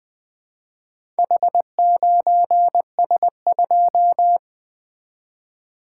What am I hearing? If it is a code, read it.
Morse code, heard as a keyed tone, about 20 wpm, 700 Hz. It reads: H9S2
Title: Four Letter-Number Combinations - 20wpm